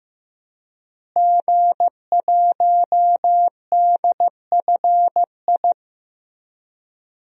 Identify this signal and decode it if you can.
Morse code, heard as a keyed tone, about 16 wpm, 700 Hz. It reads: G1DFI
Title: DXCC-100 Call Signs - 5 Characters - Repeated 3x - 15wpm